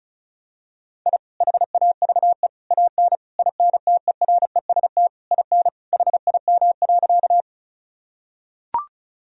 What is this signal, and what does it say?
Morse code, heard as a keyed tone, about 34 wpm, 700 Hz. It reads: I HAVE AN INTEREST IN HIM.
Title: Sentences from Top 1000 Words - Review Part 2 - 35wpm